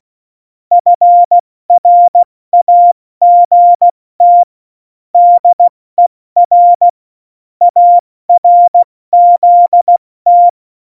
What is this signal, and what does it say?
Morse code, heard as a keyed tone, about 16 wpm, 700 Hz. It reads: FRAGT DER ARZT
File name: Morsekurs / Witze / Witz 13 March 2018, 70 F80